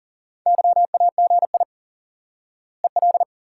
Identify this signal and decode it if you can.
Morse code, heard as a keyed tone, about 37 wpm, 700 Hz. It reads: YAGI EL
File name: IQE Course 3 - Lesson 89 - YAGI - Two Elements - 40wpm